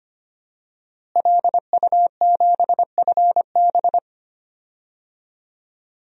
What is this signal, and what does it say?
Morse code, heard as a keyed tone, about 25 wpm, 700 Hz. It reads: LU7FB
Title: DXCC-100 Call Signs - 5 Characters - Repeated 3x - 25wpm